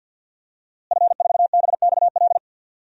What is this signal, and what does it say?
Morse code, heard as a keyed tone, about 52 wpm, 700 Hz. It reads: F4BXL
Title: DXCC-100 Call Signs - 5 Characters - Repeated 3x - 50wpm